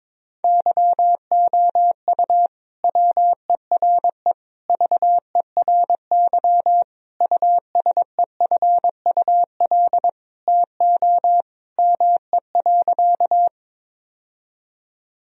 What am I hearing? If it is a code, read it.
Morse code, heard as a keyed tone, about 21 wpm, 700 Hz. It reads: YOU WERE VERY USEFUL TO ME.
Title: Sentences from Top 1000 Words - Review Part 4 - 22wpm